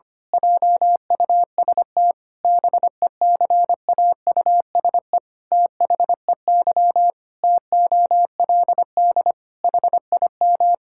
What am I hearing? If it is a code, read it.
Morse code, heard as a keyed tone, about 25 wpm, 700 Hz. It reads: JUST BECAUSE THEY TOLD HIM
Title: Pangram Challenge v2 - 25wpm